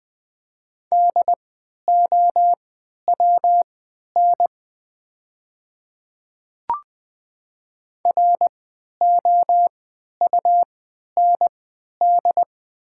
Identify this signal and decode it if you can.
Morse code, heard as a keyed tone, about 20 wpm, spaced as for 11 wpm, 700 Hz. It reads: DOWN ROUND
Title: 3x Character Spacing - Top 200 Words - 20wpm